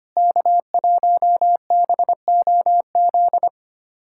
Morse code, heard as a keyed tone, about 25 wpm, 700 Hz. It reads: K1BOZ
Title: USA Callsign DE Callsign - 5 Characters - 25wpm